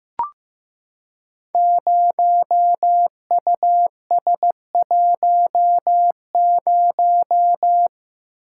Morse code, heard as a keyed tone, about 15 wpm, 700 Hz. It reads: 0US10
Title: Code Groups - 5 Letter:Numbers - 15wpm